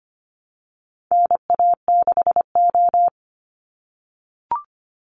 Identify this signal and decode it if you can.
Morse code, heard as a keyed tone, about 25 wpm, 700 Hz. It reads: NA6O
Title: NA SOTA Chaser Callsigns 25wpm